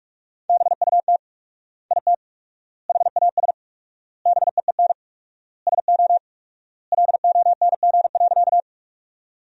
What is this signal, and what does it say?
Morse code, heard as a keyed tone, about 46 wpm, 700 Hz. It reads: BUT IT HAS BEEN SO LONG.
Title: 4x Word-Spacing - Sentences from Top 100 Words - 45wpm